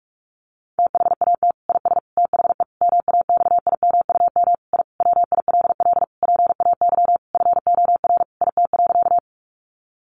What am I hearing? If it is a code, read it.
Morse code, heard as a keyed tone, about 44 wpm, 700 Hz. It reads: THAT IS THE MAXIMUM I WILL PAY FOR IT.
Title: Sentences from Top 1300 Words 45wpm